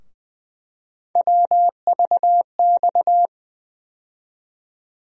Morse code, heard as a keyed tone, about 21 wpm, 700 Hz. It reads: WVX
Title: USA Call Sign Postfixes - 3 Letters 20wpm